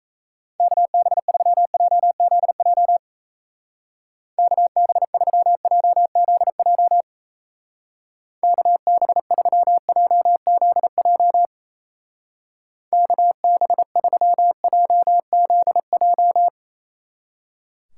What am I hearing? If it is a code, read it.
Morse code, heard as a keyed tone, about 33 wpm, 700 Hz. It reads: KB3JZJ KB3JZJ KB3JZJ KB3JZJ
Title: Speed Racing - USA 2x3 Call Signs 28wpm